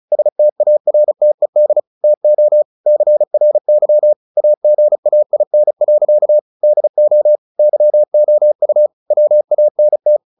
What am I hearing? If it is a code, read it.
Morse code, heard as a keyed tone, about 34 wpm, 600 Hz. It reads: STARTED TO CRY AGAIN. DO YOU WANT
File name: the old man and the sea 2 35wpm